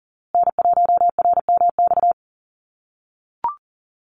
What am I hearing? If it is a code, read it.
Morse code, heard as a keyed tone, about 40 wpm, 700 Hz. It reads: N1RMX
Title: USA Call Signs Repeated 2x 40wpm